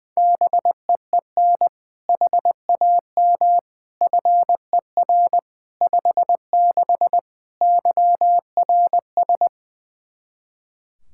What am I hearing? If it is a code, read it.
Morse code, heard as a keyed tone, about 20 wpm, 700 Hz. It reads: BEEN HAM FER 56 YRS
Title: Practice QSO - Exchange 3 - More Info - 20wpm